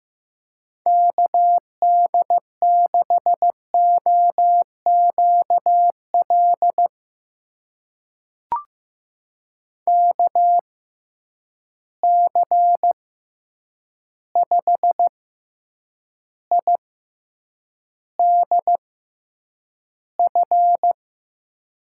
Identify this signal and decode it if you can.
Morse code, heard as a keyed tone, about 15 wpm, spaced as for 8 wpm, 700 Hz. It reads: KD6OQL KC5IDF
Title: USA Call Signs - 6 Characters - 6x Character Spacing - 15wpm